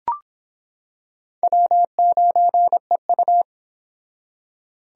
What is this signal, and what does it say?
Morse code, heard as a keyed tone, about 25 wpm, 700 Hz. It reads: W9EU